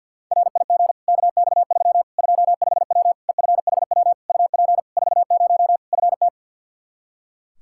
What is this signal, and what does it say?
Morse code, heard as a keyed tone, about 50 wpm, 700 Hz. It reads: RIG KX3 25W EFHW UP 40 FT
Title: Practice QSO - Exchange 2 - Info - 50wpm